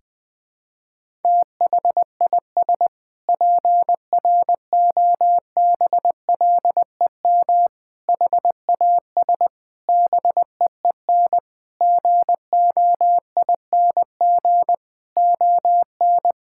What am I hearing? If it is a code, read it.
Morse code, heard as a keyed tone, about 20 wpm, 700 Hz. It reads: THIS PROBLEM HAS BEEN GOING ON
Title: Sentences with 8-Word Phrases - 20wpm